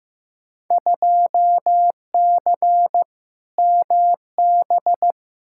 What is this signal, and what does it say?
Morse code, heard as a keyed tone, about 15 wpm, 700 Hz. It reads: 2C MB
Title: Field Day Exchanges Encore - 15wpm